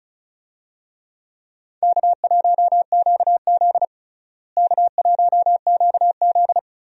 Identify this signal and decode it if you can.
Morse code, heard as a keyed tone, about 34 wpm, 700 Hz. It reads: K1QZ K1QZ
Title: USA Call Signs - 4 Characters - Repeated 3x - 35wpm